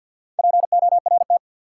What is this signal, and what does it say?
Morse code, heard as a keyed tone, about 46 wpm, 700 Hz. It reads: PORT
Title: Sentences from Top 1000 Words - Review Part 1 - New Word Drill - 50wpm